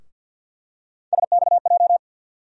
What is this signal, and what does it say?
Morse code, heard as a keyed tone, about 53 wpm, 700 Hz. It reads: SXJ
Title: USA Call Sign Postfixes - 3 Letters 50wpm